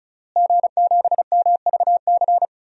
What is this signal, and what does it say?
Morse code, heard as a keyed tone, about 34 wpm, 700 Hz. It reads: G7MVC
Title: DXCC-100 Call Signs - 5 Characters - Repeated 3x - 35wpm